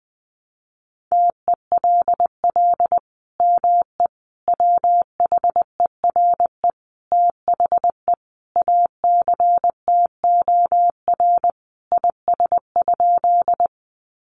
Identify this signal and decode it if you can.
Morse code, heard as a keyed tone, about 20 wpm, 700 Hz. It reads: TELL ME WHERE THE ACTOR IS?
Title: Sentences from Top 1500 Words 20wpm